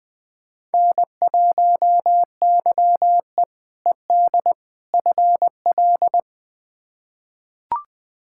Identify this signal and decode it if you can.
Morse code, heard as a keyed tone, about 20 wpm, 700 Hz. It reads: N1YE ED FL
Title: K1USN SST QSO Practice - 20wpm